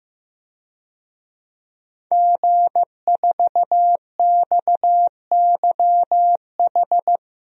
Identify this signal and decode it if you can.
Morse code, heard as a keyed tone, about 15 wpm, 700 Hz. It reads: G4XYH
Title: DXCC-100 Call Signs - 5 Characters - 15wpm